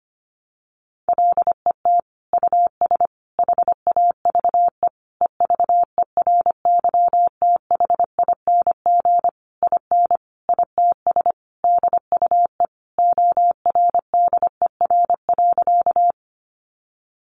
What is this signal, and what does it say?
Morse code, heard as a keyed tone, about 25 wpm, 700 Hz. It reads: LET US HAVE EVERYTHING IN ITS DUE ORDER.
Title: Sentences from Top 900 Words 25wpm